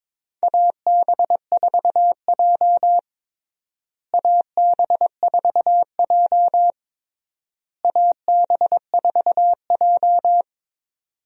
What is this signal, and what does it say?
Morse code, heard as a keyed tone, about 22 wpm, 700 Hz. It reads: AB4J AB4J AB4J
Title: USA Call Signs - 4 Characters - Repeated 3x - 22wpm